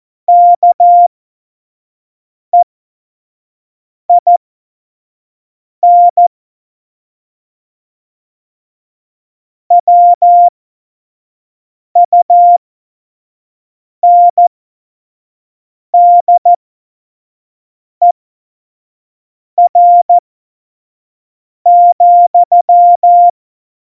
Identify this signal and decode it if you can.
Morse code, heard as a keyed tone, about 14 wpm, spaced as for 5 wpm, 700 Hz. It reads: KEIN WUNDER,
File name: Morsekurs / Witze / Witz 24. September 2019, 25 F70